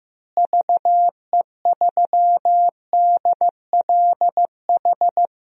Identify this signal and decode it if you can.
Morse code, heard as a keyed tone, about 15 wpm, 700 Hz. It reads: VE3DLH
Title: DXCC-100 Call Signs - 6 Characters - Repeated 2x - 15wpm